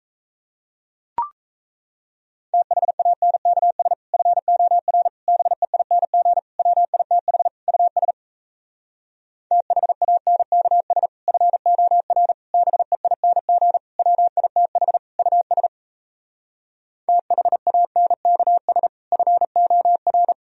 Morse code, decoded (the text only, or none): THANKS FOR BEING WITH US THANKS FOR BEING WITH US THANKS FOR